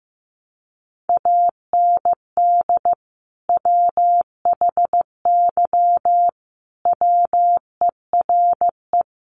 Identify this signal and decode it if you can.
Morse code, heard as a keyed tone, about 15 wpm, 700 Hz. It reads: AND WHY WERE
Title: Sentences from Top 1000 Words 15wpm